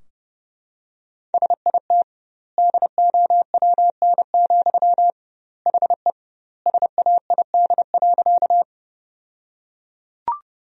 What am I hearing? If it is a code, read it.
Morse code, heard as a keyed tone, about 30 wpm, 700 Hz. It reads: SIT DOWN, HE SAID.
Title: 2x Word-Spacing - Sentences from Top 500 Words - 30wpm